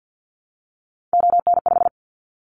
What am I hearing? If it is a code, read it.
Morse code, heard as a keyed tone, about 50 wpm, 700 Hz. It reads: ON5
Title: Callsign Prefixes for Top 50 DXCC - 50wpm